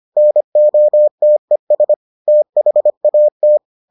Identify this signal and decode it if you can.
Morse code, heard as a keyed tone, about 25 wpm, 600 Hz. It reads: NOTES THAT